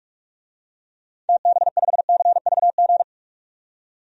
Morse code, heard as a keyed tone, about 46 wpm, 700 Hz. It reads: TB5CVG
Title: DXCC-100 Call Signs - 6 Characters - 45wpm